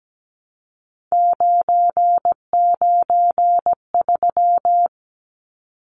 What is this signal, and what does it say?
Morse code, heard as a keyed tone, about 17 wpm, 700 Hz. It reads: 993
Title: Numbers - 3-Digit 17wpm